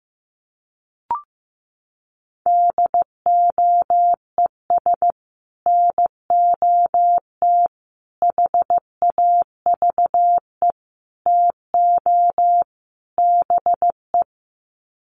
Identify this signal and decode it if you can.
Morse code, heard as a keyed tone, about 15 wpm, 700 Hz. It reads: DOES NOT HAVE TO BE
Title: Sets of 5 Words 15wpm